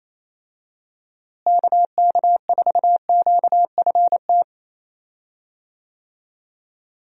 Morse code, heard as a keyed tone, about 27 wpm, 700 Hz. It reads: KK4QFT
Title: USA Call Signs - 6 Characters - Repeated 3x - 28wpm